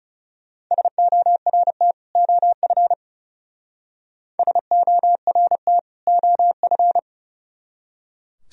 Morse code, heard as a keyed tone, about 37 wpm, 700 Hz. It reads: SORT OF SORT OF
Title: Speed-Racing - Sets of 2 Words 30wpm